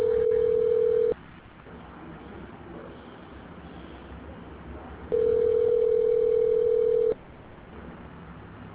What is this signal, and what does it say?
Ringback tone (North America), 440 + 480 Hz.